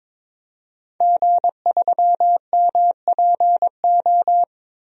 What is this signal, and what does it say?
Morse code, heard as a keyed tone, about 22 wpm, 700 Hz. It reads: G3MPO
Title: DXCC-100 Call Signs - 5 Characters - 22wpm